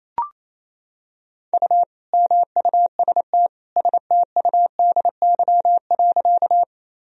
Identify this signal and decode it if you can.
Morse code, heard as a keyed tone, about 28 wpm, 700 Hz. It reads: U MUST STUDY.
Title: Sentences from Top 300 Words and Abbreviations 28wpm